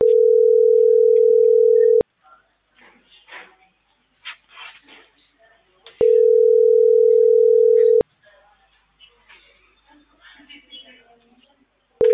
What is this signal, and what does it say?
Ringback tone (North America), 440 + 480 Hz.